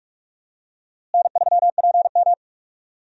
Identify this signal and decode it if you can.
Morse code, heard as a keyed tone, about 46 wpm, 700 Hz. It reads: N3PM
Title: USA Call Signs - 4 Characters - Repeated 2x - 45wpm